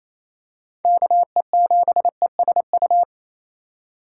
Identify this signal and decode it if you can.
Morse code, heard as a keyed tone, about 28 wpm, 700 Hz. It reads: KE7ESU
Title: USA Call Signs - 6 Characters - 3x Character Spacing - 28wpm